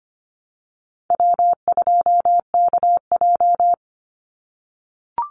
Morse code, heard as a keyed tone, about 25 wpm, 700 Hz. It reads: W2KJ
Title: Easy Call Signs - 25wpm